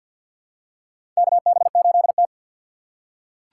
Morse code, heard as a keyed tone, about 47 wpm, 700 Hz. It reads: KB8T
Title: USA Call Signs - 2x Character Spacing - 50wpm